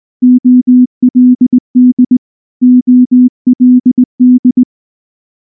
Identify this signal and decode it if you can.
Morse code, heard as a keyed tone, about 19 wpm, 250 Hz. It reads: OLD OLD
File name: morse / morse-training-20WPM-3